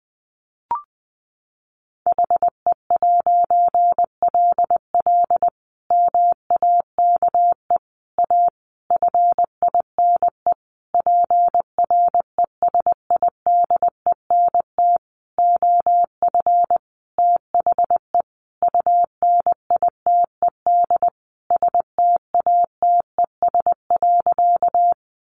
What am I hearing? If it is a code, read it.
Morse code, heard as a keyed tone, about 20 wpm, 700 Hz. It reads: HE'LL MAKE A FINE PRESIDENT OF THE UNITED STATES.
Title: A Sound Of Thunder 20wpm